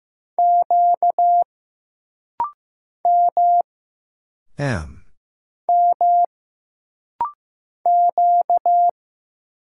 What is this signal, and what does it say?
Morse code, heard as a keyed tone, about 15 wpm, 700 Hz. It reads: Q M M Q
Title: Single Letter-Number Alt - Rapid-Fire 15wpm